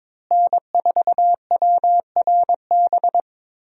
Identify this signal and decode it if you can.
Morse code, heard as a keyed tone, about 22 wpm, 700 Hz. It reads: N4WRB
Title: USA Call Signs - 5 Characters - Repeated 3x - 22wpm